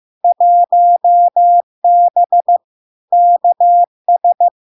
Morse code, heard as a keyed tone, about 15 wpm, 700 Hz. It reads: 1B KS
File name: LARK-FD-2022-Log-in-CW-15wpm-700-Hz